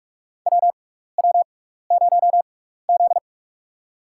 Morse code, heard as a keyed tone, about 42 wpm, spaced as for 15 wpm, 700 Hz. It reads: WW0Z
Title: USA Call Signs - 6x Character Spacing - 45wpm